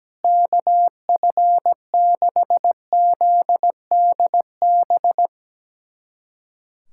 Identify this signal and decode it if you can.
Morse code, heard as a keyed tone, about 17 wpm, 700 Hz. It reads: KF6ZDB
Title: USA Callsign DE Callsign - 6 Characters - 17wpm